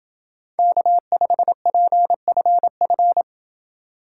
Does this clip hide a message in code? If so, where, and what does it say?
Yes: Morse code from 0.6 to 3.2 s, heard as a keyed tone, about 28 wpm, 700 Hz. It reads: K5PFF